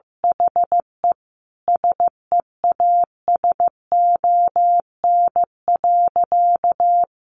Morse code, heard as a keyed tone, about 15 wpm, 700 Hz. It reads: HE SEASON.
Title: Sentences from Top 1400 Words 15wpm